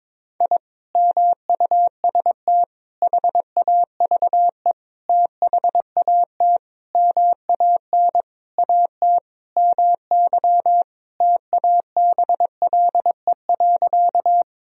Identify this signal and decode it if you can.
Morse code, heard as a keyed tone, about 22 wpm, 700 Hz. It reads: I MUST HAVE THAT MAN AT MY TABLE.
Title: Sentences from Top 1000 Words - Review Part 2 - 22wpm